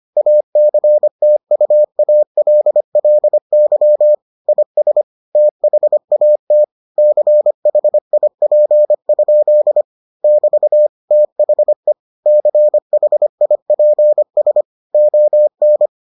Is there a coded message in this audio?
Yes: Morse code from 0.2 to 15.9 s, heard as a keyed tone, about 25 wpm, 600 Hz. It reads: ACTUALLY IS THAT CHIP? <BT> THE CHIPS ON